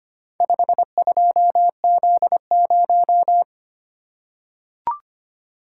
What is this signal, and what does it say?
Morse code, heard as a keyed tone, about 25 wpm, 700 Hz. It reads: 52Z0
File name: practice / 25wpm / Four Letter-Number Combinations - 25wpm